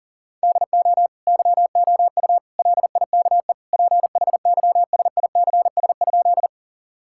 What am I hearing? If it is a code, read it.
Morse code, heard as a keyed tone, about 40 wpm, 700 Hz. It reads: DO YOU LIKE PHYSICS?
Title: Sentences from Top 1000 Words - Review Part 5 - 40wpm